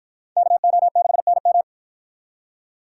Morse code, heard as a keyed tone, about 47 wpm, 700 Hz. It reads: KO6NM